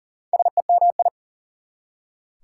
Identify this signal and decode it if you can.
Morse code, heard as a keyed tone, about 40 wpm, 700 Hz. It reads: SEMI
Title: Prefixes and Suffixes 40wpm